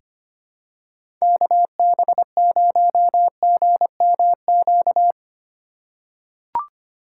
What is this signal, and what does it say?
Morse code, heard as a keyed tone, about 25 wpm, 700 Hz. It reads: KB0GMQ
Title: USA Call Signs - 6 Characters - 4x Character Spacing - 25wpm